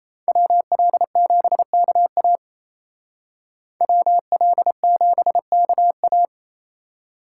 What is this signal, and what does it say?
Morse code, heard as a keyed tone, about 36 wpm, 700 Hz. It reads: WL7KA WL7KA